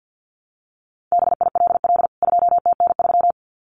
Morse code, heard as a keyed tone, about 48 wpm, 700 Hz. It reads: BILL 2TN3
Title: CWT Exchange 50wpm